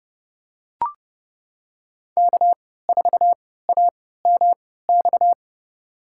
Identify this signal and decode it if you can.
Morse code, heard as a keyed tone, about 30 wpm, spaced as for 17 wpm, 700 Hz. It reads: K4AMX
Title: USA Call Signs - 5 Characters - 3x Character Spacing - 30wpm